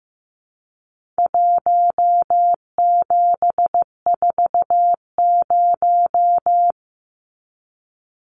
Morse code, heard as a keyed tone, about 15 wpm, 700 Hz. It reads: 1740
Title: Numbers - 4-Digit 15wpm